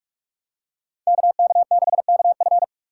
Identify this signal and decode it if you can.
Morse code, heard as a keyed tone, about 46 wpm, 700 Hz. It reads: KK6KF
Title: USA Call Signs - 5 Characters - 4x Character Spacing - 45wpm